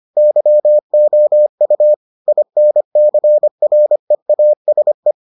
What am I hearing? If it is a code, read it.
Morse code, heard as a keyed tone, about 25 wpm, 600 Hz. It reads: YOU INCREASE